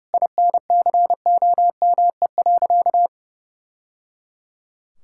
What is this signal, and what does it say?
Morse code, heard as a keyed tone, about 30 wpm, 700 Hz. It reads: INCOME.
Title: Sentences from Top 1000 Words - Review Part 6 - 30wpm